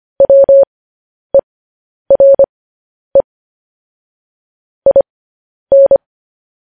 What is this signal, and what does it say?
Morse code, heard as a keyed tone, about 25 wpm, spaced as for 10 wpm, 550 Hz. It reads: WERE IN
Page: News Headlines in Morse Code at 10 WPM